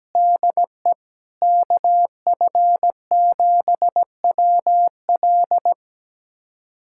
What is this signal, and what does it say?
Morse code, heard as a keyed tone, about 17 wpm, 700 Hz. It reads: DE KF7WL
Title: USA Callsign DE Callsign - 5 Characters - 17wpm